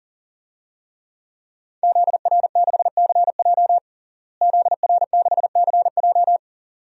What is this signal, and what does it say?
Morse code, heard as a keyed tone, about 37 wpm, 700 Hz. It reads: ZR6CJ ZR6CJ
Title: DXCC-100 Call Signs - 5 Characters - Repeated 2x - 40wpm